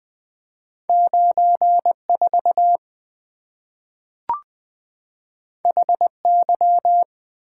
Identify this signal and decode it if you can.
Morse code, heard as a keyed tone, about 20 wpm, 700 Hz. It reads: 9 4 H Y
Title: Two Letter-Number Combinations - 20wpm